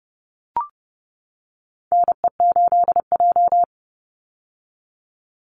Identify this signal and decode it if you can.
Morse code, heard as a keyed tone, about 30 wpm, 700 Hz. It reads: NE8J
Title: Call Signs - 30wpm